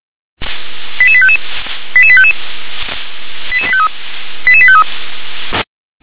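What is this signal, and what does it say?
Five-tone selective call (ZVEI), heard as quick runs of five tones, tones of about 70 ms. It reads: 88855 88855 88853 88853
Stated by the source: ZVEI five-tone selective call 88747 received by SDR.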